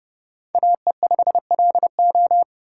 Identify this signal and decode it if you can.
Morse code, heard as a keyed tone, about 30 wpm, 700 Hz. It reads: AE5LO
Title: USA Call Signs - 5 Characters - 4x Character Spacing - 30wpm